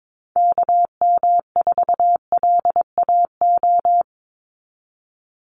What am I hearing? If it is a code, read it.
Morse code, heard as a keyed tone, about 22 wpm, 700 Hz. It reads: KM4LAO
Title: Call Signs Repeated 3x - 22wpm